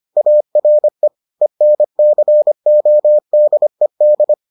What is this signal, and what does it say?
Morse code, heard as a keyed tone, about 25 wpm, 600 Hz. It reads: ARE ENCODED